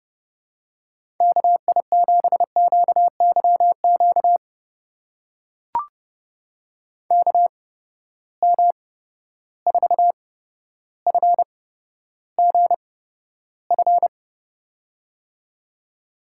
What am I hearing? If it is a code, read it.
Morse code, heard as a keyed tone, about 32 wpm, spaced as for 13 wpm, 700 Hz. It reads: KI7QYQ KM4FGF
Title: USA Call Signs - 6 Characters - 8x Character Spacing - 30wpm